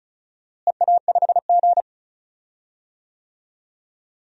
Morse code, heard as a keyed tone, about 36 wpm, 700 Hz. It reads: EA5G
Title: DXCC-100 Call Signs - 4 Characters - Repeated 3x - 35wpm